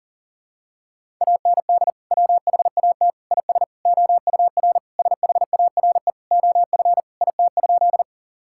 Morse code, heard as a keyed tone, about 40 wpm, 700 Hz. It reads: AND WHAT IS OUR SHARE OF IT?
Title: Sentences from Top 1000 Words - Review Part 6 - 40wpm